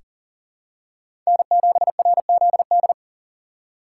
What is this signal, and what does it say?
Morse code, heard as a keyed tone, about 40 wpm, 700 Hz. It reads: N7RZD
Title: USA Call Signs - 5 Characters - 6x Character Spacing - 40wpm